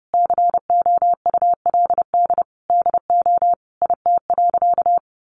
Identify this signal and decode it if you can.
Morse code, heard as a keyed tone, about 30 wpm, 700 Hz. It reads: COULD DO IT.
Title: Sentences from Top 1100 Words 30wpm